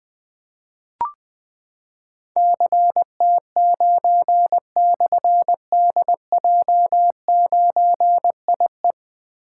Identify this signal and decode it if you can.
Morse code, heard as a keyed tone, about 20 wpm, 700 Hz. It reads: CT9/DJ9IE
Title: Difficult Call Signs 20wpm